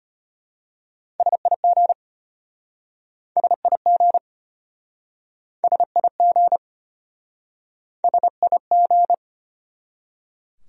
Morse code, heard as a keyed tone, about 32 wpm, 700 Hz. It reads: SIG SIG SIG SIG